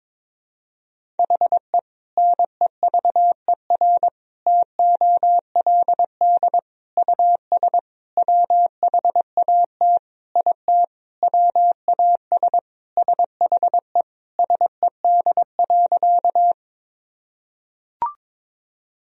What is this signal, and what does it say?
Morse code, heard as a keyed tone, about 22 wpm, 700 Hz. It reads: HE NEVER TOLD US WHAT IT WAS SHE SED.
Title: Sentences from Top 400 Words and Abbreviations 22wpm